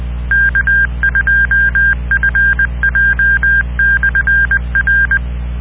Morse code, heard as a keyed tone, about 20 wpm, 1600 Hz. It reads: K2FJ/R